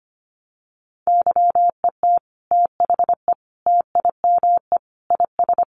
Morse code, heard as a keyed tone, about 25 wpm, 700 Hz. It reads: YET THE TIME IS
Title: Sentences from Top 1100 Words 25wpm